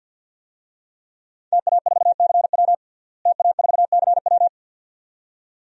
Morse code, heard as a keyed tone, about 50 wpm, 700 Hz. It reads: TA4CW TA4CW
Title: DXCC-100 Call Signs - 5 Characters - Repeated 2x - 50wpm